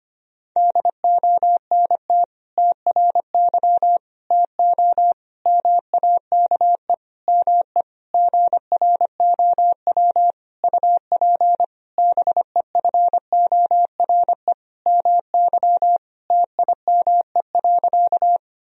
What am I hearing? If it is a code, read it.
Morse code, heard as a keyed tone, about 25 wpm, 700 Hz. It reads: DONT TRY TO MAKE ME GROW UP BEFORE MY TIME.
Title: Sentences from Top 1000 Words - Review Part 1 - 25wpm